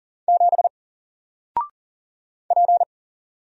Morse code, heard as a keyed tone, about 38 wpm, 700 Hz. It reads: 7 P
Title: Single Character (Letter:Number:Punctuation:Prosign) 40wpm